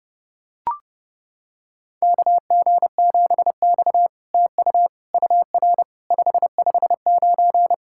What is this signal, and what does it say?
Morse code, heard as a keyed tone, about 30 wpm, 700 Hz. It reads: KG7X TU UR 559
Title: SOTA QSO Practice 30wpm